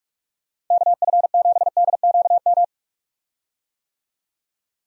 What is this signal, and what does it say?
Morse code, heard as a keyed tone, about 46 wpm, 700 Hz. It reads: KF7DQM
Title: USA Call Signs - 6 Characters - 45wpm